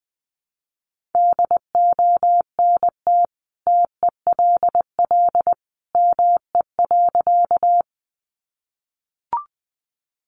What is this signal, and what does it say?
Morse code, heard as a keyed tone, about 20 wpm, 700 Hz. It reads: DONT TELL ME.
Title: Sentences from Top 300 Words 20wpm